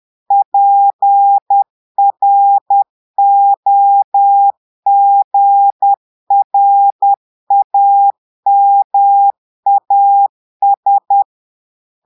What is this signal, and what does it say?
Morse code, heard as a keyed tone, about 10 wpm, 800 Hz. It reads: PROGRAMAS
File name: portugais texte en clair 2 vitesse 10 mots minutes